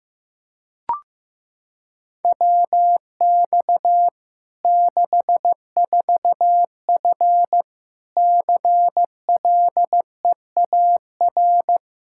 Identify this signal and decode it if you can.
Morse code, heard as a keyed tone, about 15 wpm, 700 Hz. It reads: WX 64F CLEAR
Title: Practice QSO - Exchange 2 - Info - 15wpm